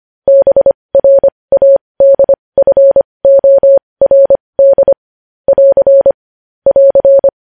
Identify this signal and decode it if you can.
Morse code, heard as a keyed tone, about 25 wpm, spaced as for 20 wpm, 550 Hz. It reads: BRADFORD <AR> <AR>
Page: news-headlines-in-morse-code-at-20-wpm